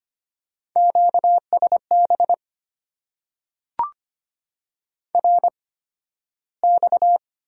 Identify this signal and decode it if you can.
Morse code, heard as a keyed tone, about 25 wpm, spaced as for 13 wpm, 690 Hz. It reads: QSB RX
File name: Top 90 QSO Elements - 8x Character Spacing - 25wpm